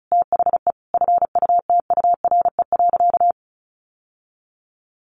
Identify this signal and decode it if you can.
Morse code, heard as a keyed tone, about 34 wpm, 700 Hz. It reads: THE FUTURE.
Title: Sentences from Top 1200 Words 35wpm